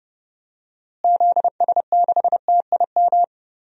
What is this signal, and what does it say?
Morse code, heard as a keyed tone, about 30 wpm, 700 Hz. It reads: ZS6TIM
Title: DXCC-100 Call Signs - 6 Characters - Repeated 2x - 30wpm